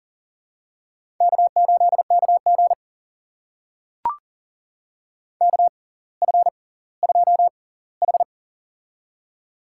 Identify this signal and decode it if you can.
Morse code, heard as a keyed tone, about 40 wpm, spaced as for 20 wpm, 700 Hz. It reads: K8KG KF2H
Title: USA Call Signs - 4 Characters - 6x Character Spacing - 40wpm